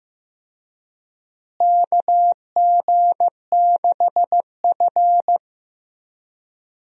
Morse code, heard as a keyed tone, about 15 wpm, 700 Hz. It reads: KG6F
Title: USA Call Signs - 4 Characters - Repeated 2x - 15wpm